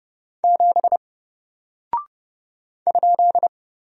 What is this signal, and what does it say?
Morse code, heard as a keyed tone, about 30 wpm, 700 Hz. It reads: 7 ?
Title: Single Character (Letter:Number:Punctuation:Prosign) 30wpm